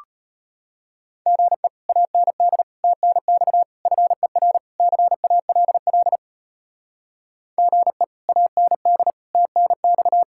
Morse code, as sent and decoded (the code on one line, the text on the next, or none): --. / .- -. -.. / - -. -..- / ..-. .-. / -.-. .- .-.. .-.. / --. / .- -. -.. / - -. -..-
GE AND TNX FER CALL GE AND TNX